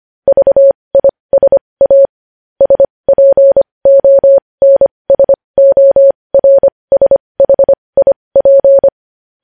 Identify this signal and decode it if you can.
Morse code, heard as a keyed tone, about 25 wpm, spaced as for 20 wpm, 550 Hz. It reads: VISA SPONSORSHIP
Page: news headlines in morse code at 20 wpm